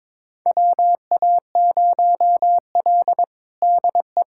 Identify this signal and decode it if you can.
Morse code, heard as a keyed tone, about 22 wpm, 700 Hz. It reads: WA0L DE
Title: USA Callsign DE Callsign - 4 Characters - 22wpm